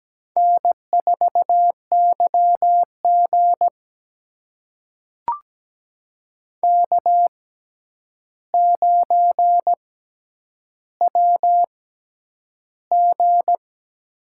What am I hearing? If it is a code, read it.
Morse code, heard as a keyed tone, about 17 wpm, spaced as for 9 wpm, 700 Hz. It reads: N4YG K9WG
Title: USA Call Signs - 4 Characters - 6x Character Spacing - 17wpm